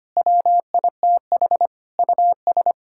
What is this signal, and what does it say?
Morse code, heard as a keyed tone, about 25 wpm, 700 Hz. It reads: WITH US
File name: Sets of 9 Words - 25wpm